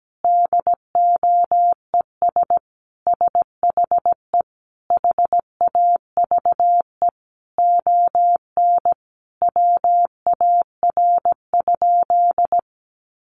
Morse code, heard as a keyed tone, about 17 wpm, 700 Hz. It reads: DOES SHE HAVE ON WAR?
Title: Sentences from Top 1200 Words 17wpm